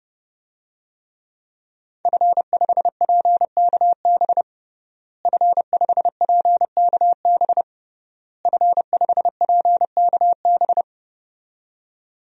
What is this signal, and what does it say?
Morse code, heard as a keyed tone, about 31 wpm, 700 Hz. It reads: F5PKB F5PKB F5PKB
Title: DXCC-100 Call Signs - 5 Characters - Repeated 3x - 30wpm